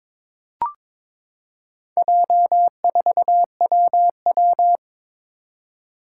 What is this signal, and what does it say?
Morse code, heard as a keyed tone, about 22 wpm, 700 Hz. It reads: J4WW
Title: Four Challenging Letter-Number Combinations 22wpm